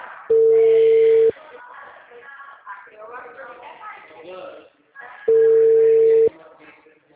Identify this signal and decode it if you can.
Ringback tone (China, 450 Hz), with speech.
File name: out- -2010- - - .45426